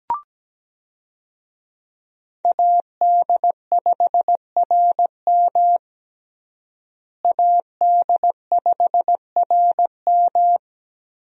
Morse code, heard as a keyed tone, about 17 wpm, 700 Hz. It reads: AD5RM AD5RM
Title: USA Call Signs - 5 Characters - Repeated 2x - 17wpm